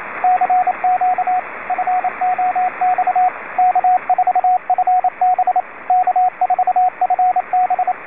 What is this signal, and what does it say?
Morse code, heard as a keyed tone, about 27 wpm, 730 Hz, with radio noise. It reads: CQ FOX K4FB K4FB